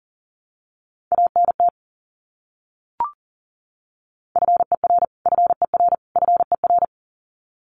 Morse code, heard as a keyed tone, about 40 wpm, 700 Hz. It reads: ANT FER FER FER
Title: Abbreviations Repeated 3x - 40wpm